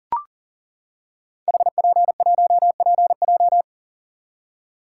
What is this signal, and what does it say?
Morse code, heard as a keyed tone, about 38 wpm, 690 Hz. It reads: HP1PJ
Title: Code Groups - 5 Letter:Numbers - 40wpm